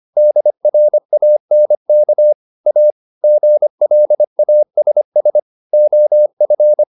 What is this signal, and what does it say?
Morse code, heard as a keyed tone, about 25 wpm, 600 Hz. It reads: DRANK A GLASS OF